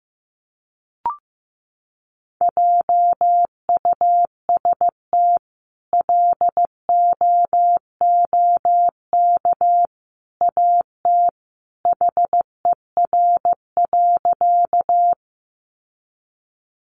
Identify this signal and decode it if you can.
Morse code, heard as a keyed tone, about 15 wpm, 700 Hz. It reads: JUST LOOK AT HER.
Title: Sentences from Top 200 Words 15wpm